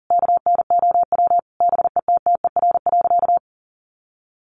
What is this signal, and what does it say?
Morse code, heard as a keyed tone, about 40 wpm, 700 Hz. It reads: KNOW BETTER.
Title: Sentences from Top 1400 Words 40wpm